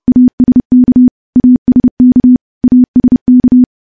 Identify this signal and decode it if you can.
Morse code, heard as a keyed tone, about 31 wpm, 250 Hz. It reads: ASK ASK ASK
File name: morse / morse-training-30WPM